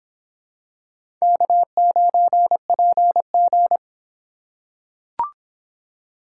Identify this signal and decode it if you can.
Morse code, heard as a keyed tone, about 25 wpm, 700 Hz. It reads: K9PG